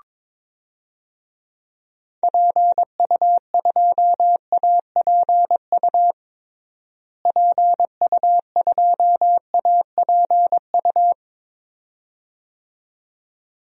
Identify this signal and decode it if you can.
Morse code, heard as a keyed tone, about 22 wpm, 700 Hz. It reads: PU2APU PU2APU
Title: DXCC-100 Call Signs - 6 Characters - Repeated 2x - 22wpm